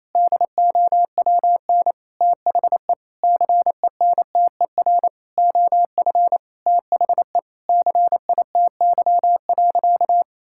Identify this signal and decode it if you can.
Morse code, heard as a keyed tone, about 28 wpm, 700 Hz. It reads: DOWN THE CENTER OF THE CITY.
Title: Sentences from Top 1000 Words - Review Part 6 - 28wpm